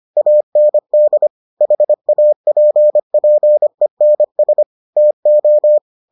Morse code, heard as a keyed tone, about 25 wpm, 600 Hz. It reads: AND HAPPENS TO